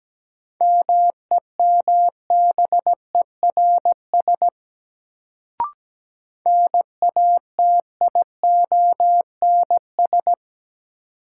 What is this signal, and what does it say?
Morse code, heard as a keyed tone, about 17 wpm, 700 Hz. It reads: MEMBERS NATIONS
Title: Sets of 3 Words - New Word Drill - 17wpm